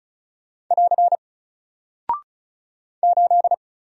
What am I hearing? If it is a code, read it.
Morse code, heard as a keyed tone, about 35 wpm, 700 Hz. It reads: <AR> 8
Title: Single Character (Letter:Number:Punctuation:Prosign) 35wpm